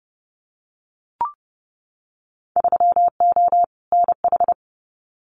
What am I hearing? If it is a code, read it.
Morse code, heard as a keyed tone, about 29 wpm, 700 Hz. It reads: 3O NH
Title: Winter Field Day Exchanges Encore 30wpm